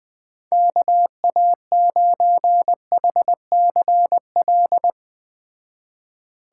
Morse code, heard as a keyed tone, about 20 wpm, 700 Hz. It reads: KA9HCL
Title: Speed Racing - USA 2x3 Call Signs 17wpm